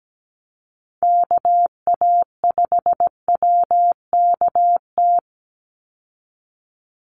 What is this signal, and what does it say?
Morse code, heard as a keyed tone, about 17 wpm, 700 Hz. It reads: KA5WKT
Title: USA Call Signs 17wpm